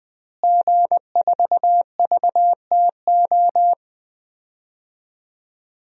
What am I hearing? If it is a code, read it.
Morse code, heard as a keyed tone, about 21 wpm, 700 Hz. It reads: G4VTO
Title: DXCC-100 Call Signs - 5 Characters - Repeated 3x - 20wpm